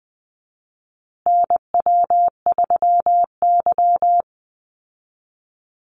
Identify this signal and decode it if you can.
Morse code, heard as a keyed tone, about 20 wpm, 700 Hz. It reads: NW3Y
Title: Call Signs Repeated 2x - 20wpm